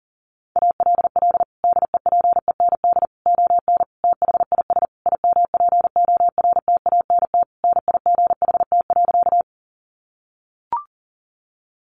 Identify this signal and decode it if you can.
Morse code, heard as a keyed tone, about 40 wpm, 700 Hz. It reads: ALL DEPEND ON THIS IMPORTANT NIGHT.
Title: Sentences from Top 1300 Words 40wpm